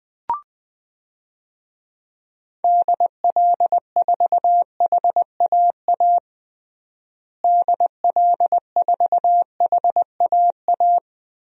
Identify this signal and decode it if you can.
Morse code, heard as a keyed tone, about 20 wpm, 700 Hz. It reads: DL4HAA DL4HAA
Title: DXCC-100 Call Signs - 6 Characters - Repeated 3x - 20wpm